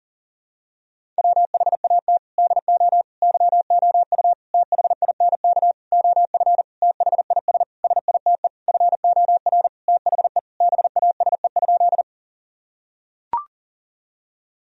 Morse code, heard as a keyed tone, about 40 wpm, 700 Hz. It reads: WHAT DO YOU THINK OF THIS SITE FOR THE BASE?
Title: Sentences from Top 1000 Words - Review Part 1 - 40wpm